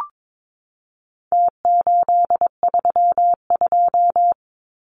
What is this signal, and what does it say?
Morse code, heard as a keyed tone, about 22 wpm, 700 Hz. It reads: T832
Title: Numbers - Contest Style - 4-Digit 22wpm